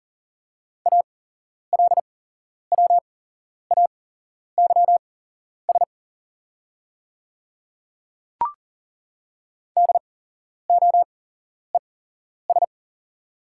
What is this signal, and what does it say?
Morse code, heard as a keyed tone, about 40 wpm, spaced as for 11 wpm, 690 Hz. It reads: ALWAYS DOES
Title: 8x Character Spacing - Top 500 Words - 40wpm